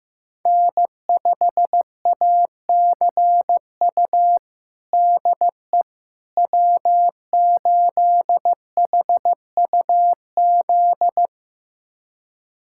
Morse code, heard as a keyed tone, about 15 wpm, 700 Hz. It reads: N5ACU DE W8HUZ
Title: USA Callsign DE Callsign - 5 Characters - 15wpm